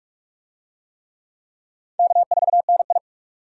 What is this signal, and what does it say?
Morse code, heard as a keyed tone, about 39 wpm, 700 Hz. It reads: K4NI